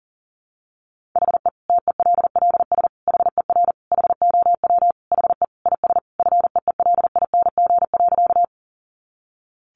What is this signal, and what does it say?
Morse code, heard as a keyed tone, about 37 wpm, 700 Hz. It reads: HE TELLS HER HOW HE IS FEELING.
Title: Additional Easy Phrases 40wpm